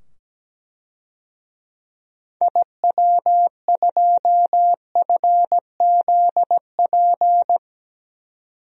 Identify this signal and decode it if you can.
Morse code, heard as a keyed tone, about 17 wpm, 700 Hz. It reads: IW2FZP
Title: DXCC-100 Call Signs - 6 Characters - 17wpm